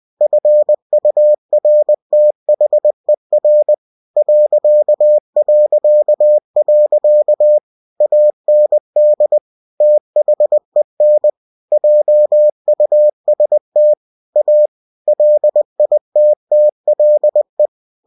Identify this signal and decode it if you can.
Morse code, heard as a keyed tone, about 20 wpm, 600 Hz. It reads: FURTHER ... AND THEN JUST A LITTLE
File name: cw-books / winnie-the-pooh / winnie-20wpm-page-0003